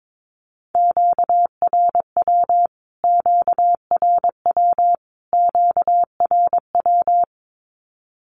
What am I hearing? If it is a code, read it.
Morse code, heard as a keyed tone, about 22 wpm, 700 Hz. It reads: QRW QRW QRW
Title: Q-Codes Repeated 3x 22wpm